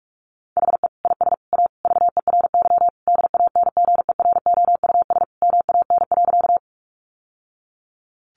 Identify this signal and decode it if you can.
Morse code, heard as a keyed tone, about 44 wpm, 700 Hz. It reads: HE IS A VERY DANGEROUS MAN.
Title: Sentences from Top 1200 Words 45wpm